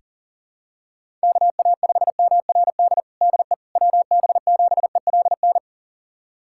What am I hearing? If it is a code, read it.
Morse code, heard as a keyed tone, about 37 wpm, 700 Hz. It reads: KA5MRD DE WB7ELN
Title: USA Callsign DE Callsign 40wpm